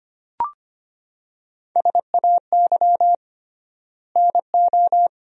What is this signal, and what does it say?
Morse code, heard as a keyed tone, about 25 wpm, 700 Hz. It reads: SAY NO
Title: 3x Word-Spacing - Sentences from Top 200 Words - 25wpm